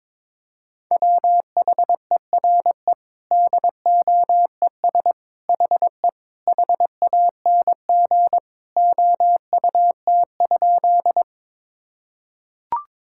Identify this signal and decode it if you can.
Morse code, heard as a keyed tone, about 22 wpm, 700 Hz. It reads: WHERE DOES HE HANG OUT?
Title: Sentences from Top 1600 Words 22wpm